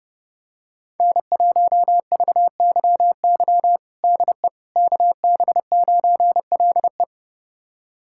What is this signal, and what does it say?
Morse code, heard as a keyed tone, about 30 wpm, 700 Hz. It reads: N1VYY DE KB9LE
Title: USA Callsign DE Callsign - 5 Characters - 30wpm